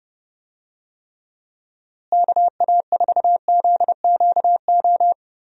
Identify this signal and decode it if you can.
Morse code, heard as a keyed tone, about 30 wpm, 700 Hz. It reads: KA4ZQO
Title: USA Call Signs - 6 Characters - Repeated 3x - 30wpm